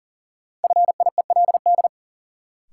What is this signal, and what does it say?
Morse code, heard as a keyed tone, about 37 wpm, 700 Hz. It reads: FIELD
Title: Sentences from Top 1000 Words - New Word Drill - 40wpm